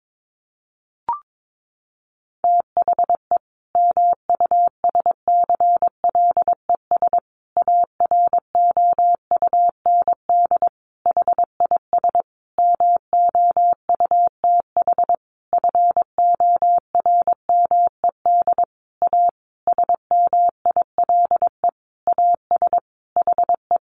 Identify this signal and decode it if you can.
Morse code, heard as a keyed tone, about 22 wpm, 700 Hz. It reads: THE MUSCLES AROUND HIS MOUTH FORMED A SMILE AS HE
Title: A Sound Of Thunder 22wpm